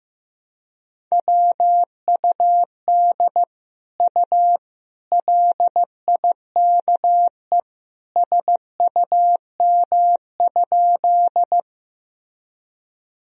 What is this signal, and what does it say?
Morse code, heard as a keyed tone, about 15 wpm, 700 Hz. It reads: WUD U LIKE SUM?
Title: Sentences from Top 100 Words and Abbreviations 15wpm